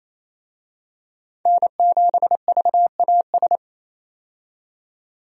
Morse code, heard as a keyed tone, about 28 wpm, 700 Hz. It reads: N7VAS
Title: USA Call Signs - 5 Characters - 28wpm